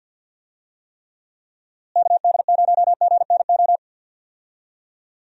Morse code, heard as a keyed tone, about 48 wpm, 700 Hz. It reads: KD0GNO
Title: USA Call Signs - 6 Characters - Repeated 2x - 50wpm